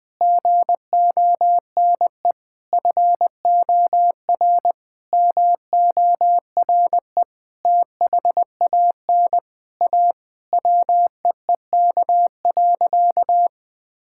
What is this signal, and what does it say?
Morse code, heard as a keyed tone, about 20 wpm, 700 Hz. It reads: GONE FOR MORE THAN A WEEK.
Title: Sentences with 4-Word Phrases - 20wpm